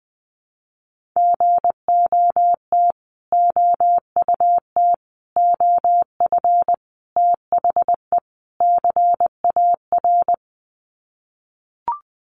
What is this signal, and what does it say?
Morse code, heard as a keyed tone, about 20 wpm, 700 Hz. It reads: GOT OUT OF THE CAR
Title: Sets of 5 Words Encore 20wpm